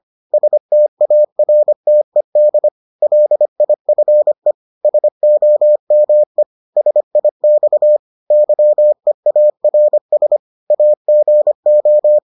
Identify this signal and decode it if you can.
Morse code, heard as a keyed tone, about 25 wpm, 600 Hz. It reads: STARTED LIFE SOME SIX YEARS AGO